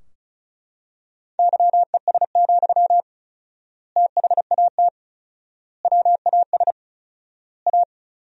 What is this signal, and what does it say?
Morse code, heard as a keyed tone, about 34 wpm, 700 Hz. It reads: YES, THAT WAS A
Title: 4x Word-Spacing - Sentences from Top 500 Words - 35wpm